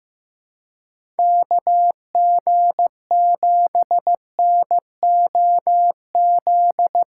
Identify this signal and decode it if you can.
Morse code, heard as a keyed tone, about 15 wpm, 700 Hz. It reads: KG7NOZ
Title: USA Call Signs - 6 Characters - 3x Character Spacing - 15wpm